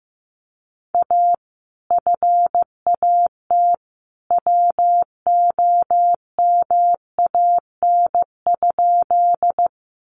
Morse code, heard as a keyed tone, about 15 wpm, 700 Hz. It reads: A FAT WOMAN?
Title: Sentences from Top 1100 Words 15wpm